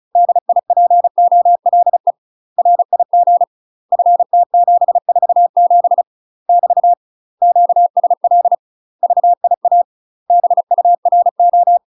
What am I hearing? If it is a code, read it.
Morse code, heard as a keyed tone, about 36 wpm, 700 Hz. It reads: DIPOLE RIG FT747 <BT> QSL VIA BURO
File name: UK MASTER QSOs TRAINING / 35WPM / UK MASTER QSOs 0038